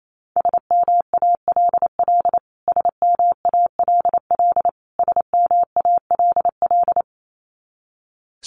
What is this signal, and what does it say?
Morse code, heard as a keyed tone, about 28 wpm, 700 Hz. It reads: SMALL SMALL SMALL
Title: Top 500 Words Repeated 28wpm